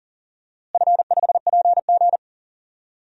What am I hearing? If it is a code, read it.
Morse code, heard as a keyed tone, about 40 wpm, 700 Hz. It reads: F5PG
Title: DXCC-100 Call Signs - 4 Characters - Repeated 2x - 40wpm